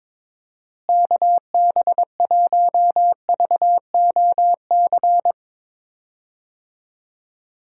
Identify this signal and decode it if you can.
Morse code, heard as a keyed tone, about 22 wpm, 700 Hz. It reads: KB1VOC
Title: USA Call Signs - 6 Characters - Repeated 2x - 22wpm